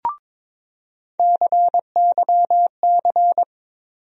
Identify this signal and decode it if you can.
Morse code, heard as a keyed tone, about 22 wpm, 700 Hz. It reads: CYC
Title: USA Call Sign Postfixes - 3 Letters 22wpm